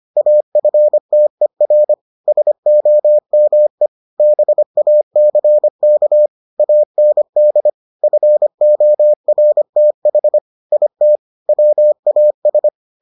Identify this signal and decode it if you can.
Morse code, heard as a keyed tone, about 25 wpm, 600 Hz. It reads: AFTER SOME BACK AND FORTH IT WAS